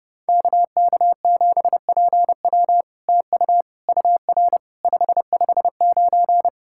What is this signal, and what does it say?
Morse code, heard as a keyed tone, about 30 wpm, 700 Hz. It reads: KK7PW TU UR 559
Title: SOTA QSO Practice 30wpm